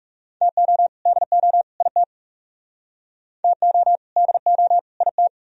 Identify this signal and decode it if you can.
Morse code, heard as a keyed tone, about 37 wpm, 700 Hz. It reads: TO DO IT TO DO IT
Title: Speed-Racing - Sets of 3 Words 30wpm